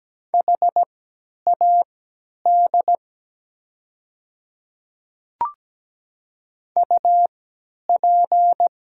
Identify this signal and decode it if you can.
Morse code, heard as a keyed tone, about 17 wpm, spaced as for 10 wpm, 700 Hz. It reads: HAD UP
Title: 3x Character Spacing - Top 300 Words - 17wpm